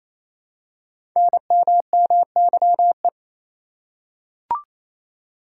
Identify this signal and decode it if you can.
Morse code, heard as a keyed tone, about 27 wpm, 700 Hz. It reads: NMMYE
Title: Code Groups - 5 Letters - 28wpm